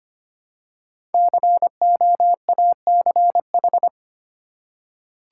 Morse code, heard as a keyed tone, about 25 wpm, 700 Hz. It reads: COACH
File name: 5 Letter Words - First 3 Letters Given - 25wpm